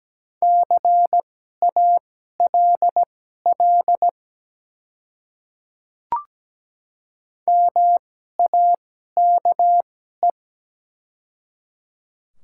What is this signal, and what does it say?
Morse code, heard as a keyed tone, about 17 wpm, spaced as for 12 wpm, 700 Hz. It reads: CALL MAKE
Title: 2x Character Spacing - Top 200 Words - 17wpm